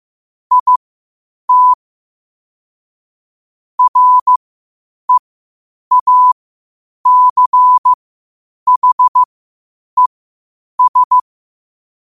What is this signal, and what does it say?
Morse code, heard as a keyed tone, about 15 wpm, spaced as for 8 wpm, 1000 Hz. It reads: IT REACHES
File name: qotdfiles / 8wpm